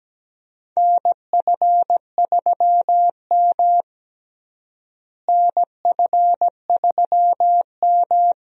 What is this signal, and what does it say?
Morse code, heard as a keyed tone, about 17 wpm, 700 Hz. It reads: NF3M NF3M
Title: USA Call Signs - 4 Characters - Repeated 2x - 17wpm